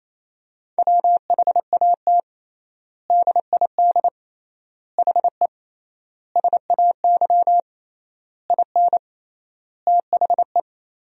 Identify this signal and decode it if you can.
Morse code, heard as a keyed tone, about 28 wpm, 700 Hz. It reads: WHAT DID HE SAY IN THE
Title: 3x Word-Spacing - Sentences from Top 300 Words - 28wpm